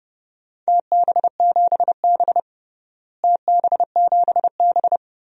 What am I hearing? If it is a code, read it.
Morse code, heard as a keyed tone, about 32 wpm, 700 Hz. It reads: TB7B TB7B
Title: DXCC-100 Call Signs - 4 Characters - Repeated 2x - 30wpm